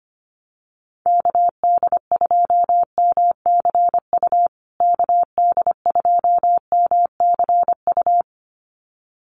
Morse code, heard as a keyed tone, about 25 wpm, 700 Hz. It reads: KD2MCU KD2MCU
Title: USA Call Signs Repeated 2x 25wpm